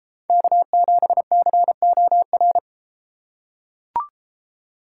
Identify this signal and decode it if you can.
Morse code, heard as a keyed tone, about 33 wpm, 700 Hz. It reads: K7COR